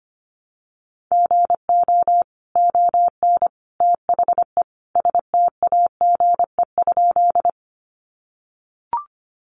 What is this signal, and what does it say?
Morse code, heard as a keyed tone, about 25 wpm, 700 Hz. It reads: GO ON THE STAGE?
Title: Sentences from Top 1200 Words 25wpm